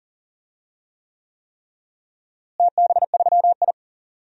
Morse code, heard as a keyed tone, about 40 wpm, 700 Hz. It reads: TB3I
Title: DXCC-100 Call Signs - 4 Characters - Repeated 3x - 40wpm